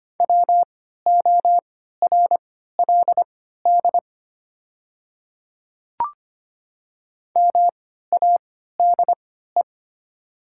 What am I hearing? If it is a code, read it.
Morse code, heard as a keyed tone, about 25 wpm, spaced as for 14 wpm, 700 Hz. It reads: WORLD MADE
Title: 3x Character Spacing - Top 200 Words - 25wpm